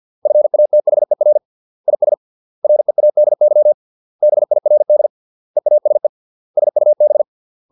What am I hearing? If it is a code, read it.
Morse code, heard as a keyed tone, about 50 wpm, 600 Hz. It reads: FATHER IS READY BIRD EASE SUB
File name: 632 words / 632 Words 50WPM